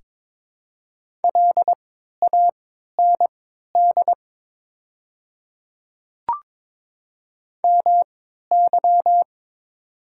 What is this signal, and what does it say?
Morse code, heard as a keyed tone, about 23 wpm, spaced as for 13 wpm, 700 Hz. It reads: LAND MY
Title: 3x Character Spacing - Top 300 Words - 22wpm